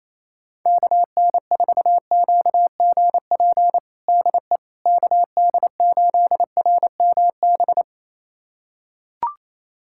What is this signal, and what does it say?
Morse code, heard as a keyed tone, about 28 wpm, 700 Hz. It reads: KN4QGP DE KD8RMB
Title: USA Callsign DE Callsign 28wpm